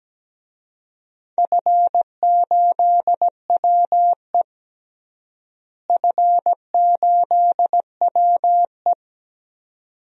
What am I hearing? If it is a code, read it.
Morse code, heard as a keyed tone, about 17 wpm, 700 Hz. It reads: F8WE F8WE
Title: DXCC-100 Call Signs - 4 Characters - Repeated 2x - 17wpm